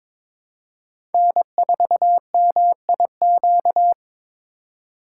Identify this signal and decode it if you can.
Morse code, heard as a keyed tone, about 26 wpm, 700 Hz. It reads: N4MIQ
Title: Speed Racing - USA 1x3 Call Signs 22wpm